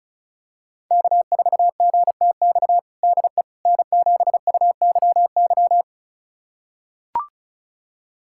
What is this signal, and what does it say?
Morse code, heard as a keyed tone, about 34 wpm, 700 Hz. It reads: K4GTX DE N7UYY
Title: USA Callsign DE Callsign 35wpm